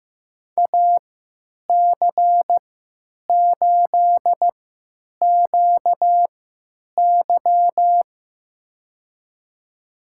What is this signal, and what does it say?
Morse code, heard as a keyed tone, about 15 wpm, spaced as for 9 wpm, 700 Hz. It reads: AC8QY
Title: USA Call Signs - 5 Characters - 3x Character Spacing - 15wpm